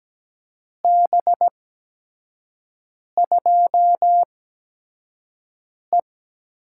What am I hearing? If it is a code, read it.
Morse code, heard as a keyed tone, about 17 wpm, spaced as for 5 wpm, 700 Hz. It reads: B2E
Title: USA Call Signs - 8x Character Spacing - 17wpm